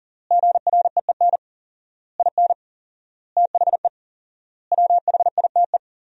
Morse code, heard as a keyed tone, about 40 wpm, 700 Hz. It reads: GREEN IN THE WHITE
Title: 4x Word-Spacing - Sentences from Top 500 Words - 40wpm